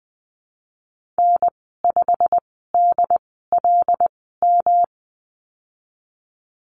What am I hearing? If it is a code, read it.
Morse code, heard as a keyed tone, about 20 wpm, spaced as for 15 wpm, 700 Hz. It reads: N5DLM
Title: 161 Lesson 26g - Letter g - Call Signs 20wpm